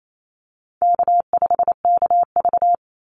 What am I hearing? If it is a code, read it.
Morse code, heard as a keyed tone, about 28 wpm, 700 Hz. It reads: K5KV
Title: Easy Call Signs - 28wpm